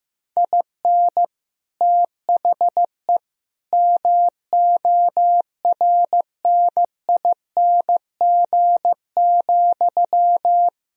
Morse code, heard as a keyed tone, about 15 wpm, 700 Hz. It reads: IN THE MORNING,
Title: Sentences from Top 1000 Words - Review Part 3 - 15wpm